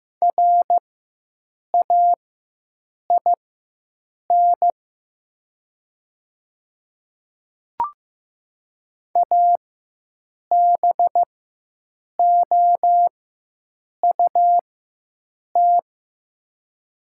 Morse code, heard as a keyed tone, about 15 wpm, spaced as for 7 wpm, 700 Hz. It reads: RAIN ABOUT
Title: 4x Character Spacing - Top 300 Words - 15wpm